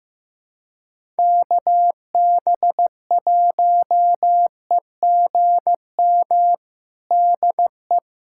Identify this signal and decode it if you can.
Morse code, heard as a keyed tone, about 15 wpm, 700 Hz. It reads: KB1EGM DE
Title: USA Callsign DE Callsign 15wpm